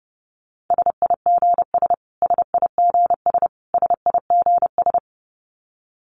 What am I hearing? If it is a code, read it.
Morse code, heard as a keyed tone, about 30 wpm, 700 Hz. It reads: SIGS SIGS SIGS
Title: Abbreviations Repeated 3x - 30wpm